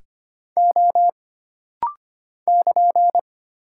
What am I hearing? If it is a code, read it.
Morse code, heard as a keyed tone, about 25 wpm, 700 Hz. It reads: O <KN>
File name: Single Character (Letter:Number:Punctuation:Prosign) - Rapid-Fire - 25wpm